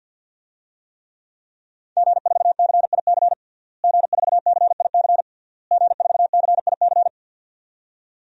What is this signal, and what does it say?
Morse code, heard as a keyed tone, about 52 wpm, 700 Hz. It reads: G4CIC G4CIC G4CIC
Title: DXCC-100 Call Signs - 5 Characters - Repeated 3x - 50wpm